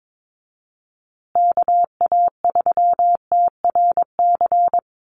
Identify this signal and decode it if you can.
Morse code, heard as a keyed tone, about 22 wpm, 700 Hz. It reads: KA3TRC
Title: USA Call Signs Repeated 3x 22wpm